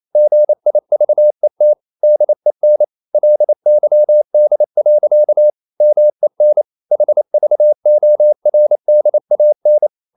Morse code, heard as a keyed tone, about 28 wpm, 600 Hz. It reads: GIVET DEN LYD. MEN HVORDAN
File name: Ferie QRQ 28wpm